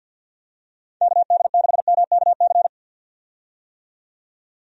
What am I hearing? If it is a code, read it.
Morse code, heard as a keyed tone, about 50 wpm, 700 Hz. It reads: KD6MKC
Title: USA Call Signs - 6 Characters - 50wpm